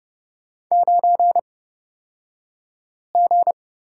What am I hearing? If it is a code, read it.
Morse code, heard as a keyed tone, about 30 wpm, spaced as for 5 wpm, 700 Hz. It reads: 9G
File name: Lesson 37d - Letter x - Two Characters 30wpm